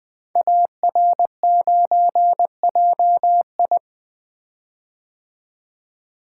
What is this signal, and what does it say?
Morse code, heard as a keyed tone, about 20 wpm, 700 Hz. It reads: AR9JI
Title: Code Groups - 5 Letter:Numbers - 20wpm